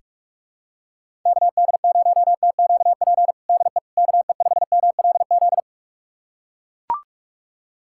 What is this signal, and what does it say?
Morse code, heard as a keyed tone, about 41 wpm, 700 Hz. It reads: KD0TQP DE KE5MLZ
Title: USA Callsign DE Callsign - 6 Characters - 45wpm